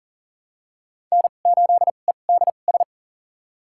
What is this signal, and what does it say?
Morse code, heard as a keyed tone, about 40 wpm, spaced as for 29 wpm, 700 Hz. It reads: N8EDS
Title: USA Call Signs - 5 Characters - 2x Character Spacing - 40wpm